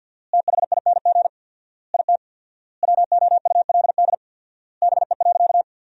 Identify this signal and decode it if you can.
Morse code, heard as a keyed tone, about 50 wpm, 700 Hz. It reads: THING IT WOULD BE.
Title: 4x Word-Spacing - Sentences from Top 200 Words - 50wpm